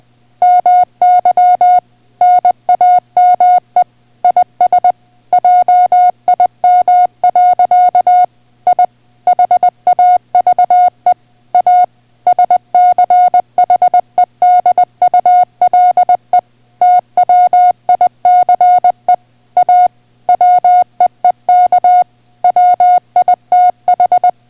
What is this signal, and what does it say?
Morse code, heard as a keyed tone, about 20 wpm, 720 Hz. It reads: MY NAME IS JIM. I HAVE A SCHEDULE TWICE A WEEK WITH